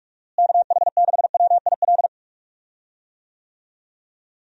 Morse code, heard as a keyed tone, about 46 wpm, 700 Hz. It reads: KH6WIL
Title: USA Call Signs - 6 Characters - Repeated 3x - 45wpm